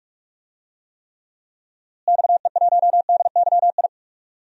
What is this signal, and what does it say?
Morse code, heard as a keyed tone, about 46 wpm, 700 Hz. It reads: XE1DYI
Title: DXCC-100 Call Signs - 6 Characters - Repeated 3x - 45wpm